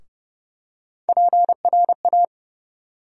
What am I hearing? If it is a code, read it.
Morse code, heard as a keyed tone, about 30 wpm, 700 Hz. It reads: PRA
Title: USA Call Sign Postfixes - 3 Letters 30wpm